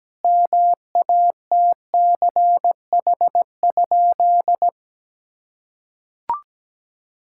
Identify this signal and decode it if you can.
Morse code, heard as a keyed tone, about 17 wpm, 700 Hz. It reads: MATCH?
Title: Sentences from Top 1600 Words 17wpm